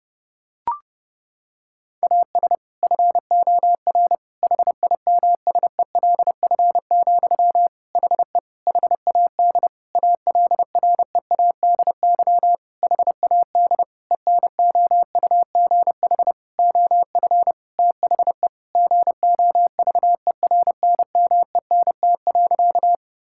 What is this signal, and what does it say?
Morse code, heard as a keyed tone, about 30 wpm, 700 Hz. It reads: AS FOR HIMSELF, HE HAD ALREADY HAD ENOUGH OF THE GOVERNMENT.
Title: Sentences from Top 1600 Words 30wpm